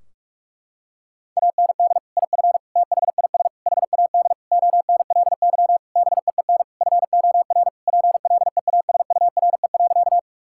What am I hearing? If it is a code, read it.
Morse code, heard as a keyed tone, about 46 wpm, 700 Hz. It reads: AND IF THIS HAD ONLY BEEN FOR PLEASURE.
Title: Sentences from Top 1600 Words 45wpm